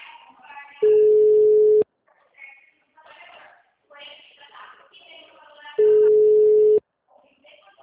Ringback tone (Europe, 425 Hz).